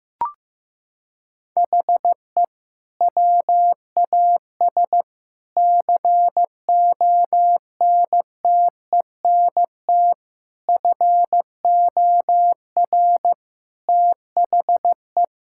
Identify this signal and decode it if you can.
Morse code, heard as a keyed tone, about 16 wpm, 700 Hz. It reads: HE WAS CONTENT FOR THE
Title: Sentences from Top 1000 Words - Review Part 6 - 15wpm